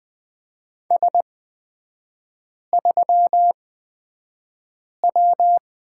Morse code, heard as a keyed tone, about 20 wpm, spaced as for 5 wpm, 700 Hz. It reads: S3W
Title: 224 Lesson 36e - BT - Three Characters 20wpm